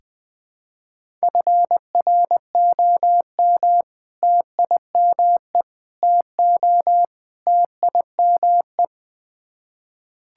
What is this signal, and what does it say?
Morse code, heard as a keyed tone, about 20 wpm, 700 Hz. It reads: FROM TIME TO TIME
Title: Build A Phrase - 4-Word Phrases - 20wpm